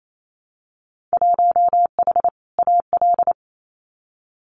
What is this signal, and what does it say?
Morse code, heard as a keyed tone, about 28 wpm, 700 Hz. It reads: 1H AL
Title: Winter Field Day Exchanges 28wpm